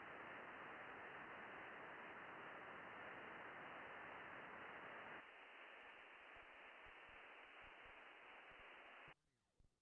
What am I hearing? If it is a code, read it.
FT8.
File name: contests / june-vhf-k5tr- -0200